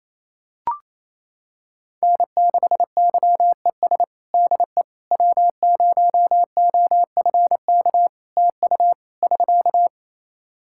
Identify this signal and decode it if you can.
Morse code, heard as a keyed tone, about 28 wpm, 700 Hz. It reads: N6YES DE W0OFK TU <SK>
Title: K1USN SST QSO Practice 28wpm